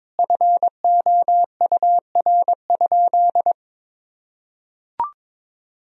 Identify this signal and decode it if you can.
Morse code, heard as a keyed tone, about 22 wpm, 700 Hz. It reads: FOUR?
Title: Sentences from Top 1000 Words - Review Part 2 - 22wpm